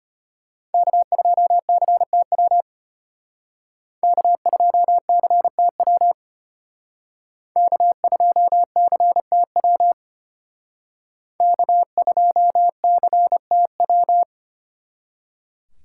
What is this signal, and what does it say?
Morse code, heard as a keyed tone, about 34 wpm, 700 Hz. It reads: K2CTW K2CTW K2CTW K2CTW